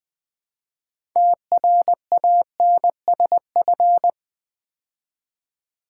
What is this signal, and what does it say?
Morse code, heard as a keyed tone, about 20 wpm, 700 Hz. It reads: TRANSF
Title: Build A Word - 3 Letters At A Time - 9-Letter Words - 20wpm